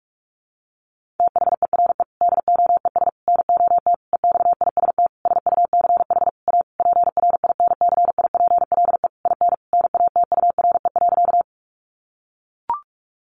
Morse code, heard as a keyed tone, about 44 wpm, 700 Hz. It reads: THERE DOES NOT EXIST SUCH A PRINCIPLE IN NATURE.
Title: Sentences from Top 1400 Words 45wpm